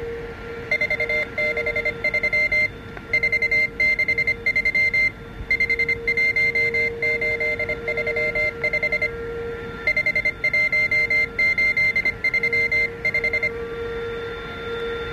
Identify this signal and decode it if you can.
Morse code, heard as a keyed tone, about 25 wpm, 2100 Hz. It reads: 463 463 51835 51835